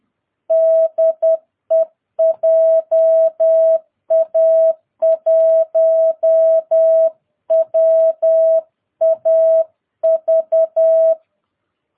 Morse code, heard as a keyed tone, about 10 wpm, 650 Hz. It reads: DE JA1WAV
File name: jh8cbh / 2022cw